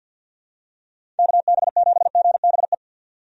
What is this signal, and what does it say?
Morse code, heard as a keyed tone, about 51 wpm, 700 Hz. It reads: KB7GBE
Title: USA Call Signs - 6 Characters - 50wpm